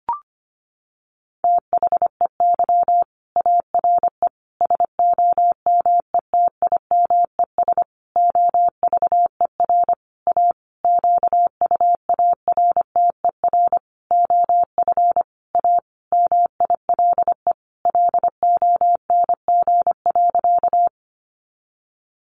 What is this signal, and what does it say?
Morse code, heard as a keyed tone, about 25 wpm, 700 Hz. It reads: THEY ARE SOMETIMES OVER A QUARTER OF A MILE LONG.
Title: Sentences from Top 1300 Words 25wpm